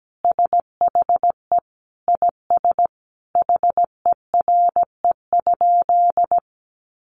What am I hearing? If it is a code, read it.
Morse code, heard as a keyed tone, about 17 wpm, 700 Hz. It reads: SHE IS HERE?
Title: Sentences from Top 1300 Words 17wpm